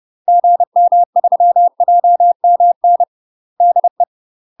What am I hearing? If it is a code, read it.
Morse code, heard as a keyed tone, about 30 wpm, 700 Hz. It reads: GM3JMN DE
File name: UK MASTER QSOs TRAINING / 30WPM / UK MASTER QSOs 0008